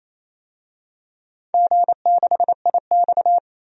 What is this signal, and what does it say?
Morse code, heard as a keyed tone, about 28 wpm, 700 Hz. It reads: G6IX
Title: DXCC-100 Call Signs - 4 Characters - 28wpm